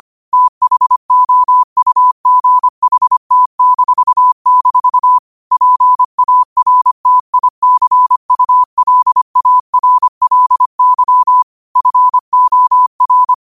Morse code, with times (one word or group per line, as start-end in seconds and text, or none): 0.3-5.2 THOUGHT--
5.5-11.4 PARTICULARLY
11.8-13.3 FOR